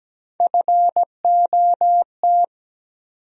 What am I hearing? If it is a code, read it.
Morse code, heard as a keyed tone, about 17 wpm, 700 Hz. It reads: FOT
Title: USA Call Sign Postfixes - 3 Letters 17wpm